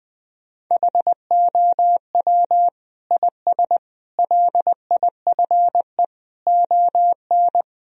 Morse code, heard as a keyed tone, about 20 wpm, 700 Hz. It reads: HOW IS LIFE ON
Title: Sentences from Top 1000 Words - Review Part 2 - 20wpm